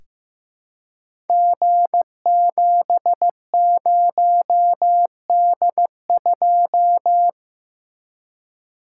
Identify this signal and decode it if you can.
Morse code, heard as a keyed tone, about 15 wpm, 700 Hz. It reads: G70D2
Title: Code Groups - 5 Letter:Numbers - 15wpm